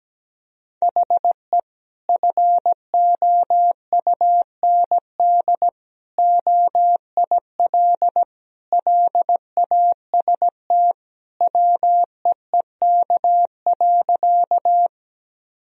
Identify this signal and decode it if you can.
Morse code, heard as a keyed tone, about 17 wpm, 700 Hz. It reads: HE FOUND OIL LAST WEEK.
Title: Sentences from Top 1000 Words - Review Part 4 - 17wpm